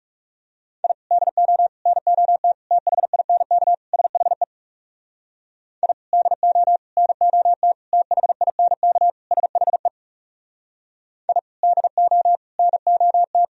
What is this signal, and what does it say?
Morse code, heard as a keyed tone, about 37 wpm, 700 Hz. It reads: I DO NOT THINK SHE I DO NOT THINK SHE I DO NOT
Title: Speed-Racing - Sets of 5 Words 30wpm